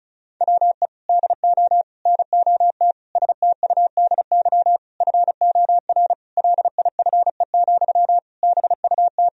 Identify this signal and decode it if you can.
Morse code, heard as a keyed tone, about 34 wpm, 700 Hz. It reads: WE DO NOT STUDY FOR LIFE, BUT
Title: Sentences from Top 1000 Words - Review Part 1 - 35wpm